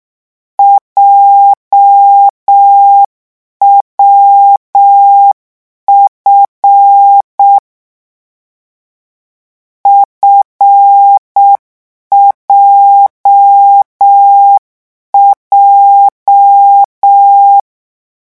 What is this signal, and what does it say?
Morse code, heard as a keyed tone, about 6 wpm, 800 Hz. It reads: JWF FJJ